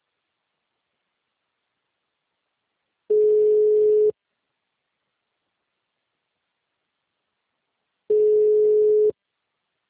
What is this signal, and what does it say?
Ringback tone (Europe, 425 Hz).